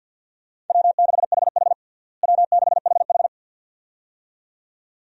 Morse code, heard as a keyed tone, about 52 wpm, 700 Hz. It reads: W6HH W6HH
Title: USA Call Signs - 4 Characters - Repeated 2x - 50wpm